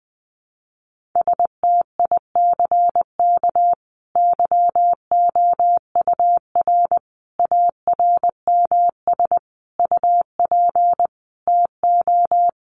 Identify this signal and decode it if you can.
Morse code, heard as a keyed tone, about 19 wpm, 700 Hz. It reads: STICK YOUR ARMS UP TO
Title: A Sound Of Thunder 20wpm